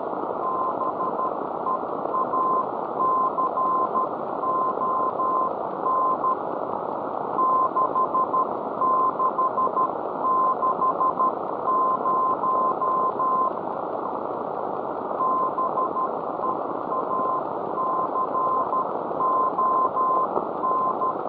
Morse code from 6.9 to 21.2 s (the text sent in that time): 6660 BEACON